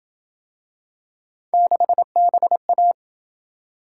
Morse code, heard as a keyed tone, about 26 wpm, 700 Hz. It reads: 6BA